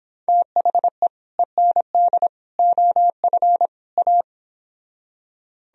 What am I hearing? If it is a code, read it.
Morse code, heard as a keyed tone, about 31 wpm, 700 Hz. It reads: THE END OF A